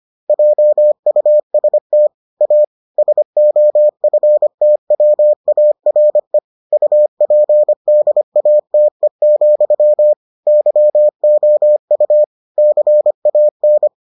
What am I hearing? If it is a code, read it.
Morse code, heard as a keyed tone, about 25 wpm, 600 Hz. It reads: JUST A SOFTWARE UPDATE, YOU CAN